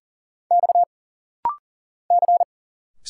Morse code, heard as a keyed tone, about 40 wpm, 700 Hz. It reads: X C
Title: Single Letters - Rapid-Fire 40wpm